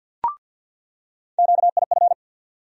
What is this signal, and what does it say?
Morse code, heard as a keyed tone, about 50 wpm, 700 Hz. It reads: QIF
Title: USA Call Sign Postfixes - 3 Letters 50wpm